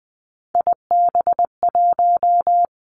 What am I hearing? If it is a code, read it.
Morse code, heard as a keyed tone, about 20 wpm, 700 Hz. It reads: IB1
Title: Three Challenging Letter-Number Combinations 20wpm